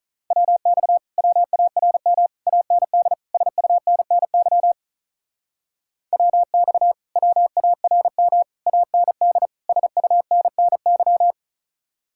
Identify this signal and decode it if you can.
Morse code, heard as a keyed tone, about 45 wpm, 700 Hz. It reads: WX WARM AND SUNNY WX WARM AND SUNNY